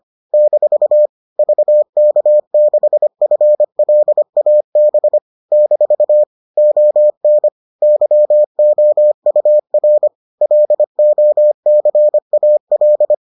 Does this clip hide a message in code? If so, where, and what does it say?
Yes: Morse code from 0.3 to 13.1 s, heard as a keyed tone, about 25 wpm, 600 Hz. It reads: - VK6FLAB - ON YOUR LOCAL